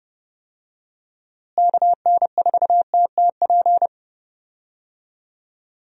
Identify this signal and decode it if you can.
Morse code, heard as a keyed tone, about 30 wpm, 700 Hz. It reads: KN4TTP
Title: USA Call Signs - 6 Characters - Repeated 2x - 30wpm